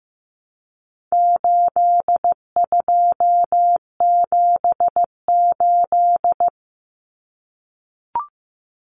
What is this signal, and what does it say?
Morse code, heard as a keyed tone, about 15 wpm, 700 Hz. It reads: 8278
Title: Numbers - 4-Digit 15wpm